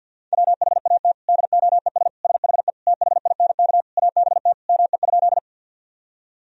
Morse code, heard as a keyed tone, about 50 wpm, 700 Hz. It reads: WHAT DOES SHE THINK ABT ME?
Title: Sentences from Top 200 Words and Abbreviations 50wpm